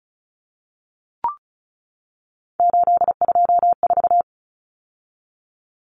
Morse code, heard as a keyed tone, about 34 wpm, 700 Hz. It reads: 824
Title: Numbers - 3-Digit 35wpm